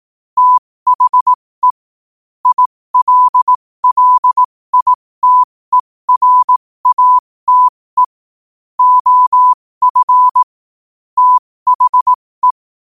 Morse code, heard as a keyed tone, about 18 wpm, spaced as for 15 wpm, 1000 Hz. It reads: THE ILLITERATE OF THE
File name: qotdfiles / 15wpm